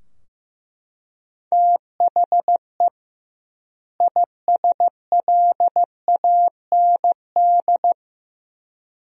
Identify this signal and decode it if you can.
Morse code, heard as a keyed tone, about 15 wpm, 700 Hz. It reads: THE ISLAND
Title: 2x Word-Spacing - Sentences from Top 500 Words - 15wpm